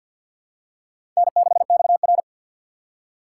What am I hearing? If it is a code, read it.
Morse code, heard as a keyed tone, about 53 wpm, 700 Hz. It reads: N6XR
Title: USA Call Signs - 4 Characters - 2x Character Spacing - 50wpm